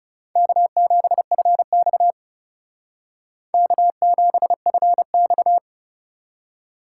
Morse code, heard as a keyed tone, about 36 wpm, 700 Hz. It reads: K7FX K7FX
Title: Speed Racing - USA 1x2 Call Signs 30wpm